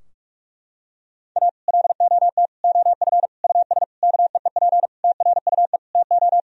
Morse code, heard as a keyed tone, about 41 wpm, 700 Hz. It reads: A LOT OF US KEEP TRUE TO
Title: Sentences from Top 1000 Words - Review Part 3 - 45wpm